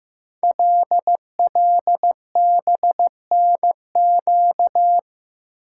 Morse code, heard as a keyed tone, about 15 wpm, 700 Hz. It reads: LLBNQ
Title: Code Groups - 5 Letters - 15wpm